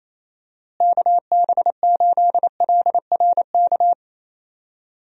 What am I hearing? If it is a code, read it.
Morse code, heard as a keyed tone, about 28 wpm, 700 Hz. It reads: KB8LRK
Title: USA Call Signs - 6 Characters - Repeated 3x - 28wpm